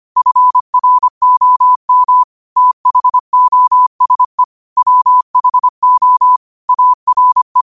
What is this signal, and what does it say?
Morse code, heard as a keyed tone, about 25 wpm, 1000 Hz. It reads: FROM THOSE WHO ARE
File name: qotdfiles / 25wpm